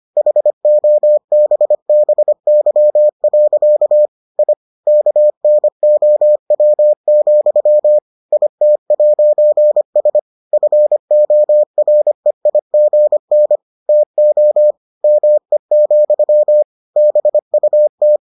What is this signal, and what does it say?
Morse code, heard as a keyed tone, about 25 wpm, 600 Hz. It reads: HOBBY. I KNOW, IT'S FOREIGN TO ME, BUT